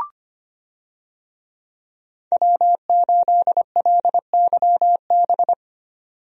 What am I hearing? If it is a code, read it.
Morse code, heard as a keyed tone, about 25 wpm, 700 Hz. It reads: W8LYB
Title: USA Call Signs - 5 Characters - 25wpm